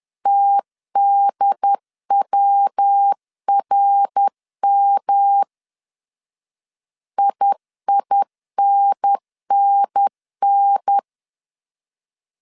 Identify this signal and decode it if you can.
Morse code, heard as a keyed tone, about 10 wpm, 800 Hz. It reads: TDWRM IINNN
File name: CW / DET / 1 3 1500C S B 10wpm 3335